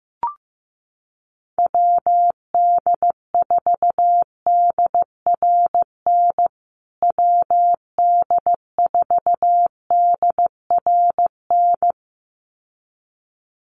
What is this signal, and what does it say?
Morse code, heard as a keyed tone, about 14 wpm, 700 Hz. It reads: WD4DRN WD4DRN
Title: USA Call Signs Repeated 2x 15wpm